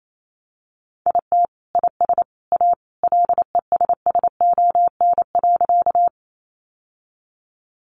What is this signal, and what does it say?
Morse code, heard as a keyed tone, about 28 wpm, 700 Hz. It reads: IT IS A LESSON.
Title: Sentences from Top 1200 Words 28wpm